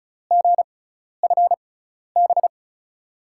Morse code, heard as a keyed tone, about 34 wpm, spaced as for 12 wpm, 700 Hz. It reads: GFB
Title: USA Call Signs - 6x Character Spacing - 35wpm